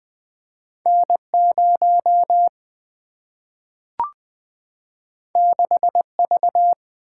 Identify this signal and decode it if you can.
Morse code, heard as a keyed tone, about 20 wpm, 700 Hz. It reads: N0 6V
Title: Single-Letter Single-Number 20wpm